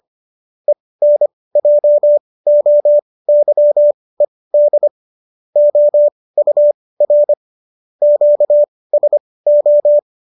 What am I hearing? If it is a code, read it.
Morse code, heard as a keyed tone, about 25 wpm, spaced as for 18 wpm, 590 Hz. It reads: ENJOYED OUR QSO
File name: qso218 18